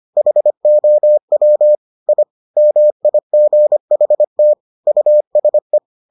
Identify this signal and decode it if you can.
Morse code, heard as a keyed tone, about 25 wpm, 600 Hz. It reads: HOW I MIGHT USE